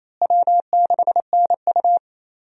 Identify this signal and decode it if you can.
Morse code, heard as a keyed tone, about 28 wpm, 700 Hz. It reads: W6NU
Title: USA Call Signs - 4 Characters - Repeated 3x - 28wpm